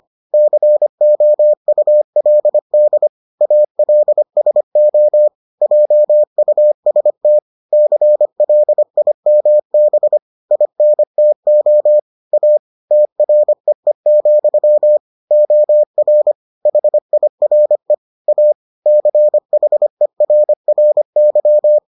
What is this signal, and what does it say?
Morse code, heard as a keyed tone, about 25 wpm, 600 Hz. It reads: COULD ALSO JUST CLIMB INTO A TREE, OR HIRE A CHERRY